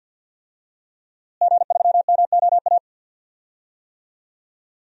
Morse code, heard as a keyed tone, about 51 wpm, 700 Hz. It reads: G3MOA
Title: DXCC-100 Call Signs - 5 Characters - Repeated 3x - 50wpm